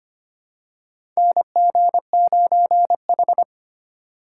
Morse code, heard as a keyed tone, about 25 wpm, 700 Hz. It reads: NG9H
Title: USA Call Signs - 4 Characters - 4x Character Spacing - 25wpm